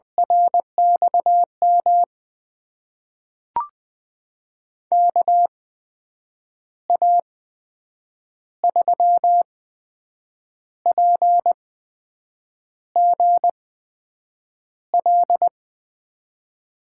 Morse code, heard as a keyed tone, about 21 wpm, spaced as for 7 wpm, 700 Hz. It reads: RXM KA3PGL
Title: USA Call Signs - 6 Characters - 8x Character Spacing - 20wpm